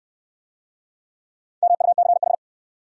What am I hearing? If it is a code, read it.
Morse code, heard as a keyed tone, about 67 wpm, 700 Hz. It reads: MUCH